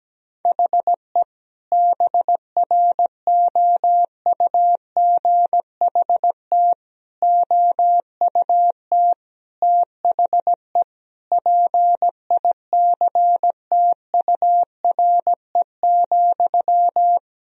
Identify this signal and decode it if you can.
Morse code, heard as a keyed tone, about 17 wpm, 700 Hz. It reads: HE BROUGHT OUT THE PICTURE,
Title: Sentences from Top 1000 Words - Review Part 3 - 17wpm